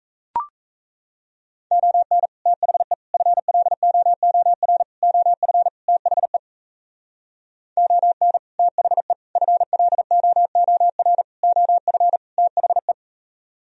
Morse code, heard as a keyed tone, about 38 wpm, 700 Hz. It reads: ON THE FLOOR OF THE ON THE FLOOR OF THE